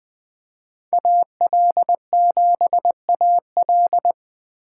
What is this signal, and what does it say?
Morse code, heard as a keyed tone, about 21 wpm, 700 Hz. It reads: AL7AL
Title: Speed Racing - USA 2x2 Call Signs 17wpm